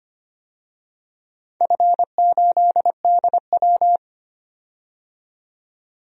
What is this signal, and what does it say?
Morse code, heard as a keyed tone, about 25 wpm, 700 Hz. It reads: F8DW
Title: DXCC-100 Call Signs - 4 Characters - 25wpm